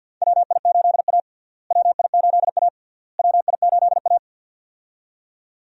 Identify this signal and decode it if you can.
Morse code, heard as a keyed tone, about 45 wpm, 700 Hz. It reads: WI8A WI8A WI8A
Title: USA Call Signs - 4 Characters - Repeated 3x - 50wpm